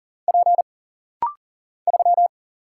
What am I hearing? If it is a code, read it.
Morse code, heard as a keyed tone, about 40 wpm, 700 Hz. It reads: P 3
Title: Single Letter-Number Alt - Rapid-Fire 40wpm